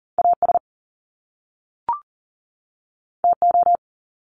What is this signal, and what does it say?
Morse code, heard as a keyed tone, about 41 wpm, 700 Hz. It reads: AS TO
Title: Top Two Letter Combinations in Words 40wpm